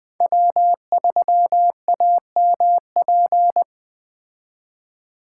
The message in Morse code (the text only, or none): W3AMP